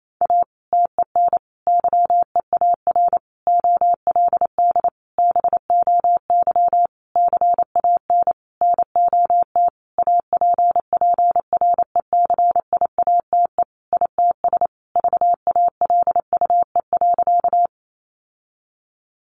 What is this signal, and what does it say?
Morse code, heard as a keyed tone, about 28 wpm, 700 Hz. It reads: A TEN YEAR OLD BOY CAN NOT APPRECIATE ITS VALUE.
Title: Sentences from Top 1300 Words 28wpm